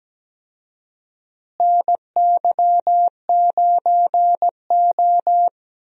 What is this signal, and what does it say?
Morse code, heard as a keyed tone, about 17 wpm, 700 Hz. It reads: NY9O
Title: USA Call Signs - 4 Characters - Repeated 3x - 17wpm